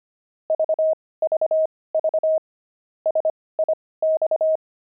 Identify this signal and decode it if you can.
Morse code, heard as a keyed tone, about 25 wpm, spaced as for 18 wpm, 630 Hz. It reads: VVV SIX
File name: pangram-fs-18-cs-25-steaks